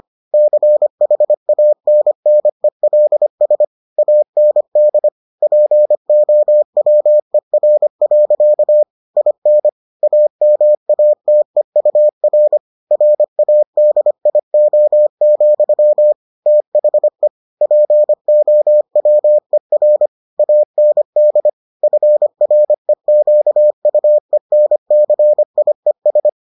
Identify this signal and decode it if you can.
Morse code, heard as a keyed tone, about 25 wpm, 600 Hz. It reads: CHANNELS AND POWER. IN AMATEUR RADIO, THE POWER AND FREQUENCIES